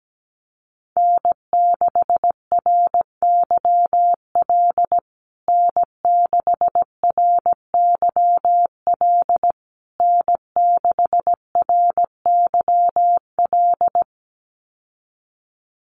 Morse code, heard as a keyed tone, about 17 wpm, 700 Hz. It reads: N6RYL N6RYL N6RYL
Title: USA Call Signs Repeated 3x 17wpm